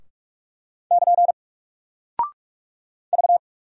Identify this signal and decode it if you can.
Morse code, heard as a keyed tone, about 46 wpm, 700 Hz. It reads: <KN> V
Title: Single Character (Letter:Number:Punctuation:Prosign) Alt 45wpm